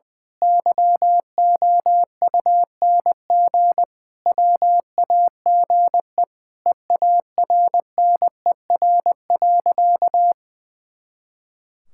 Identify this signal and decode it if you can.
Morse code, heard as a keyed tone, about 21 wpm, 700 Hz. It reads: YOUNG WAGE EARNER.
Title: Pangram Challenge v2 - 20wpm